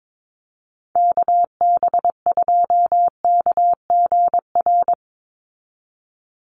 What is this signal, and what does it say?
Morse code, heard as a keyed tone, about 22 wpm, 700 Hz. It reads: KB2KGR
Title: USA Call Signs 22wpm